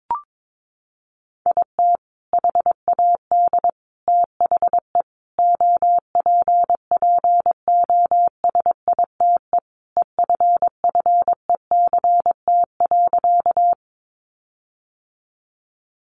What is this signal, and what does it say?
Morse code, heard as a keyed tone, about 22 wpm, 700 Hz. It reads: IT HAD THE OPPOSITE EFFECT.
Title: Sentences from Top 1300 Words 22wpm